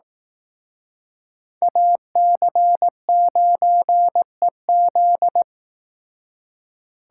Morse code, heard as a keyed tone, about 18 wpm, 700 Hz. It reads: AC9EZ
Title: Speed Racing - USA Call Signs 15wpm